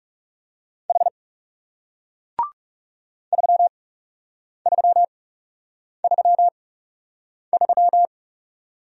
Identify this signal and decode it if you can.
Morse code, heard as a keyed tone, about 39 wpm, spaced as for 8 wpm, 700 Hz. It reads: H 3333